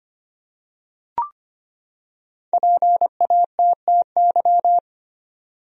Morse code, heard as a keyed tone, about 25 wpm, 700 Hz. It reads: PATTY
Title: USA Common Names - Max 5 Letters - 25wpm